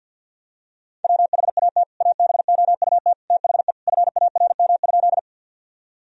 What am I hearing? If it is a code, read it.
Morse code, heard as a keyed tone, about 48 wpm, 700 Hz. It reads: WHAT ABOUT THE FARM?
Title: Sentences from Top 1000 Words - Review Part 2 - 50wpm